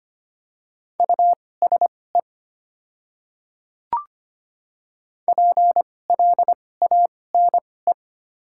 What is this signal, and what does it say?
Morse code, heard as a keyed tone, about 25 wpm, spaced as for 18 wpm, 700 Hz. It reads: USE PLANE
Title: 2x Character Spacing - Top 500 Words - 25wpm